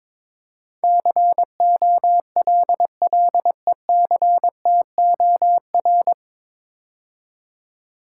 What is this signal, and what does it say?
Morse code, heard as a keyed tone, about 22 wpm, 700 Hz. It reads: COLLECTOR
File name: 9 Letter Words - First 3 Letters Given - 22wpm